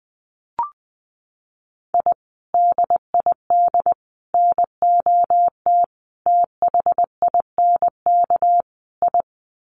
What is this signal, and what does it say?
Morse code, heard as a keyed tone, about 20 wpm, 700 Hz. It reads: I DID NOT THINK I
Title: Sets of 5 Words Encore 20wpm